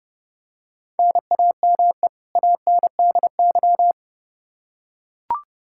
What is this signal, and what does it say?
Morse code, heard as a keyed tone, about 30 wpm, 700 Hz. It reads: NAME ANDY
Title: Practice QSO - Exchange 1 - Beginning - 30wpm